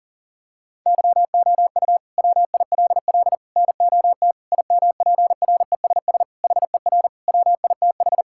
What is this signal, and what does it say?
Morse code, heard as a keyed tone, about 40 wpm, 700 Hz. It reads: YOU WILL NOT IMPRESS HER WITH
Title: Sentences from Top 1600 Words 40wpm